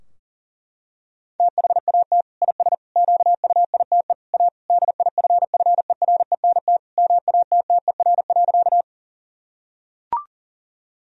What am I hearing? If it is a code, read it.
Morse code, heard as a keyed tone, about 40 wpm, 700 Hz. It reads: THAT IS QUITE A DIFFERENT MATTER.
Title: Sentences from Top 1000 Words - Review Part 3 - 40wpm